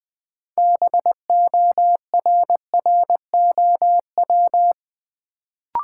Morse code, heard as a keyed tone, about 20 wpm, 700 Hz. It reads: BORROW
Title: Sentences from Top 1500 Words - New Word Drill Alt - 20wpm